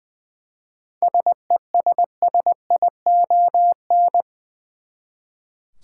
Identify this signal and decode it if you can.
Morse code, heard as a keyed tone, about 20 wpm, 700 Hz. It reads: SESSION
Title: Seven Letter Words - 20wpm